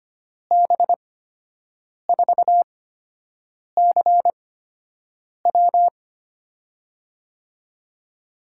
Morse code, heard as a keyed tone, about 25 wpm, spaced as for 7 wpm, 700 Hz. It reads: B4CW
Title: USA Call Signs - 8x Character Spacing - 25wpm